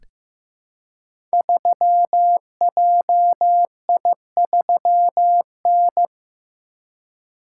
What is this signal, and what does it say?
Morse code, heard as a keyed tone, about 15 wpm, 700 Hz. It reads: 3JI3N
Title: Code Groups - 5 Letter:Numbers - 15wpm